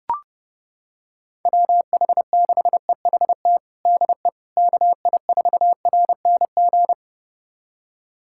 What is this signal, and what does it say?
Morse code, heard as a keyed tone, about 30 wpm, 700 Hz. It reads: WH6EHT DE KI4RNG
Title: USA Callsign DE Callsign 30wpm